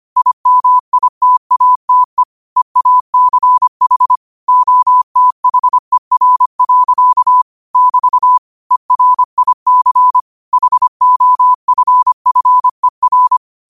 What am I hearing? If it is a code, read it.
Morse code, heard as a keyed tone, about 25 wpm, 1000 Hz. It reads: IMITATE EACH OTHER. <BT> ERIC HOFFER